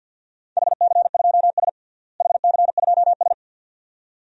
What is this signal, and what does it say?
Morse code, heard as a keyed tone, about 50 wpm, 700 Hz. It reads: HC2S HC2S
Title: DXCC-100 Call Signs - 4 Characters - Repeated 2x - 50wpm